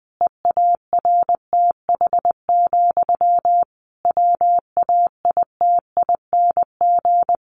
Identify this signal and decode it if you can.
Morse code, heard as a keyed tone, about 20 wpm, 700 Hz. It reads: EARTH, WAITING
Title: A Sound Of Thunder 20wpm